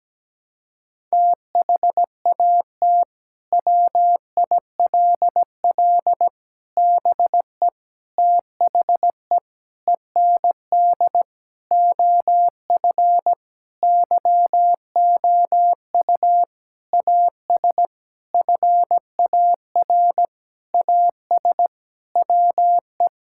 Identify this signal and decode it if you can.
Morse code, heard as a keyed tone, about 17 wpm, 700 Hz. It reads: THAT WILL BE THE END OF YOU AS FAR AS WE
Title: Sentences from Top 1000 Words - Review Part 6 - 17wpm